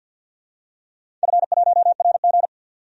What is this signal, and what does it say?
Morse code, heard as a keyed tone, about 52 wpm, 700 Hz. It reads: F1RG
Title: DXCC-100 Call Signs - 4 Characters - 50wpm